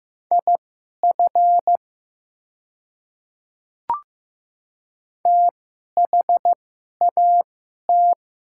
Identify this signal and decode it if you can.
Morse code, heard as a keyed tone, about 15 wpm, spaced as for 11 wpm, 700 Hz. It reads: IF THAT
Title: 2x Character Spacing - Top 100 Words - 15wpm